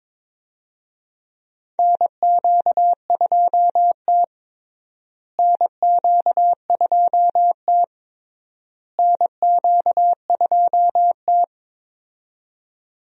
Morse code, heard as a keyed tone, about 22 wpm, 700 Hz. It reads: NQ2T NQ2T NQ2T
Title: USA Call Signs - 4 Characters - Repeated 3x - 22wpm